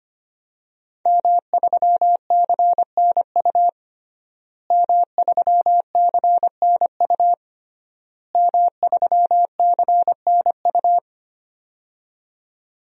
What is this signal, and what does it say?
Morse code, heard as a keyed tone, about 25 wpm, 700 Hz. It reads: M3CNU M3CNU M3CNU
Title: DXCC-100 Call Signs - 5 Characters - Repeated 3x - 25wpm